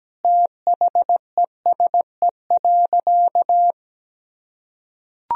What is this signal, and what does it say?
Morse code, heard as a keyed tone, about 17 wpm, 700 Hz. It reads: THESE.
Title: Sentences from Top 1000 Words - Review Part 4 - 17wpm